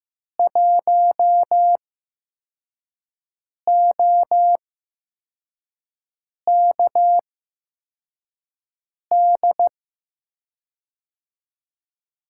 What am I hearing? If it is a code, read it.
Morse code, heard as a keyed tone, about 15 wpm, spaced as for 4 wpm, 700 Hz. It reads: 1OKD
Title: USA Call Signs - 5 Characters - 8x Character Spacing - 15wpm